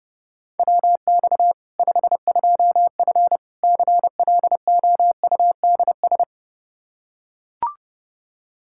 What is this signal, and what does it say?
Morse code, heard as a keyed tone, about 30 wpm, 700 Hz. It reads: WX 52F CLOUDS
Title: K1USN SST QSO Practice 30wpm